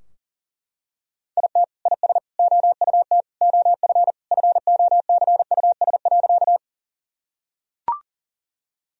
Morse code, heard as a keyed tone, about 40 wpm, 700 Hz. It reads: IT IS OUT OF FOCUS.
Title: Sentences from Top 1000 Words - Review Part 4 - 40wpm